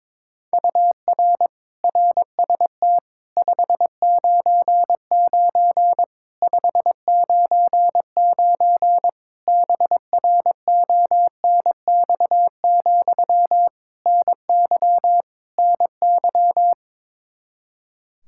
Morse code, heard as a keyed tone, about 22 wpm, 700 Hz. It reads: UR RST 599 599 BRONX, NY NY
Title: Practice QSO - Exchange 1 - Beginning - 22wpm